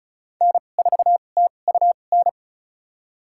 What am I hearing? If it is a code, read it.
Morse code, heard as a keyed tone, about 36 wpm, spaced as for 25 wpm, 700 Hz. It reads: N4TUN
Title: USA Call Signs - 5 Characters - 2x Character Spacing - 35wpm